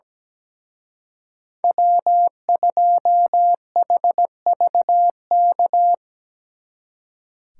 Morse code, heard as a keyed tone, about 20 wpm, 700 Hz. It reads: W2HVK
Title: Speed Racing - USA 1x3 Call Signs 17wpm